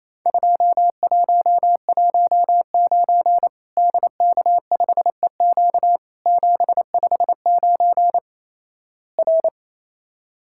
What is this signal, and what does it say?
Morse code, heard as a keyed tone, about 28 wpm, 700 Hz. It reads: 2119 DK5EQ 759 R
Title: Worked All Europe DX Contest - QTC Exchange - 28wpm